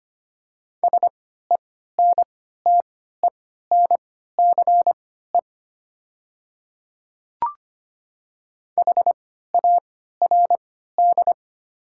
Morse code, heard as a keyed tone, about 25 wpm, spaced as for 14 wpm, 700 Hz. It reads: SENTENCE HARD
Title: 3x Character Spacing - Top 300 Words - 25wpm